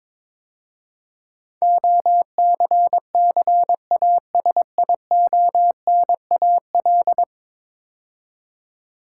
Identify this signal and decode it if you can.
Morse code, heard as a keyed tone, about 22 wpm, 700 Hz. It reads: OCCASIONAL
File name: practice / 22wpm / 10 Letter Words - First 3 Letters Given - 22wpm